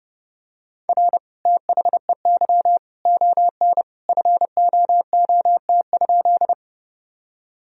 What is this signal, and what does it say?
Morse code, heard as a keyed tone, about 32 wpm, 700 Hz. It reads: R THEY ON FOOT?
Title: Sentences from Top 500 Words and Abbreviations 30wpm